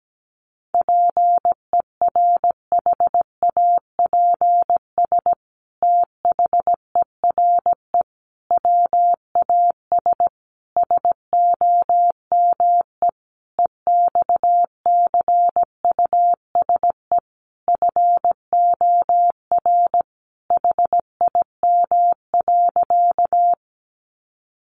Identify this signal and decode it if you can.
Morse code, heard as a keyed tone, about 17 wpm, 700 Hz. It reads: PERHAPS THERE WAS SOME EXCUSE FOR HIM.
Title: Sentences from Top 700 Words 17wpm